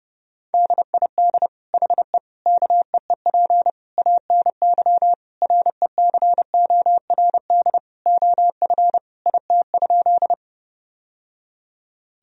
Morse code, heard as a keyed tone, about 32 wpm, 700 Hz. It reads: DID HE KEEP ANY RECORD OF IT?
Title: Sentences from Top 1000 Words - Review Part 3 - 30wpm